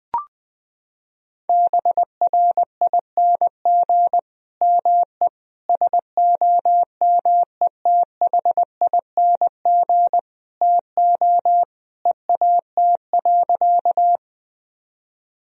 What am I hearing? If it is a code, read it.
Morse code, heard as a keyed tone, about 19 wpm, 700 Hz. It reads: BRING ME SOMETHING TO EAT.
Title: Sentences from Top 1000 Words - Review Part 3 - 20wpm